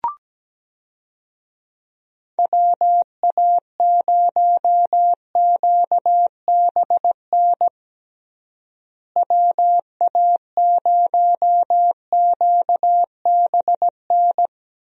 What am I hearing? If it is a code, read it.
Morse code, heard as a keyed tone, about 17 wpm, 700 Hz. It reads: WA0QBN WA0QBN
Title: USA Call Signs - 6 Characters - Repeated 3x - 17wpm